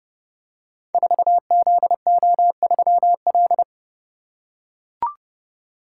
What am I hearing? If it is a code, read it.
Morse code, heard as a keyed tone, about 30 wpm, 700 Hz. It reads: 4ZO3L
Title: Code Groups - 5 Letter:Numbers - 30wpm